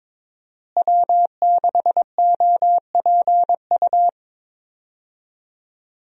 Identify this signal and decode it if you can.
Morse code, heard as a keyed tone, about 21 wpm, 700 Hz. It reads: W6OPU
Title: USA Call Signs - 5 Characters - Repeated 2x - 22wpm